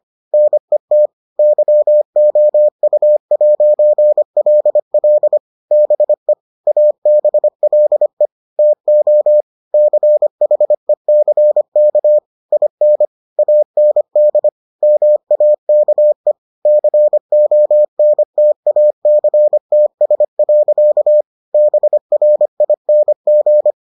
Morse code, heard as a keyed tone, about 25 wpm, 600 Hz. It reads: NET YOU'LL BE ABLE TO CHECK IN AND MAKE CONTACTS. BRING